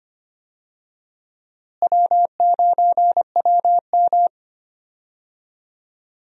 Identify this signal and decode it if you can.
Morse code, heard as a keyed tone, about 25 wpm, 700 Hz. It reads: W9WM
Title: USA Call Signs - 4 Characters - Repeated 3x - 25wpm